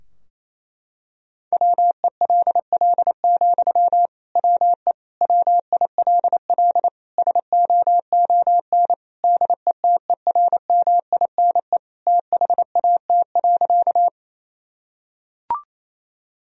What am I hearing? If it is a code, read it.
Morse code, heard as a keyed tone, about 28 wpm, 700 Hz. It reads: WELL, WE WILL SOON DETERMINE THAT.
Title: Sentences from Top 1000 Words - Review Part 4 - 28wpm